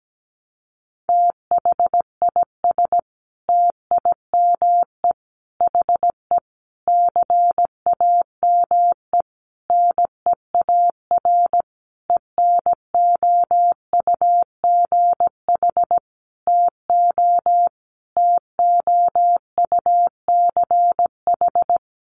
Morse code, heard as a keyed tone, about 17 wpm, 700 Hz. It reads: THIS TIME HE CAME NEAR ENOUGH TO TOUCH
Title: Sentences from Top 1200 Words 17wpm